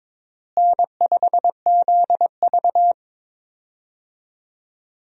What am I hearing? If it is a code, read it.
Morse code, heard as a keyed tone, about 22 wpm, 700 Hz. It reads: N5ZV
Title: USA Call Signs - 4 Characters - Repeated 3x - 22wpm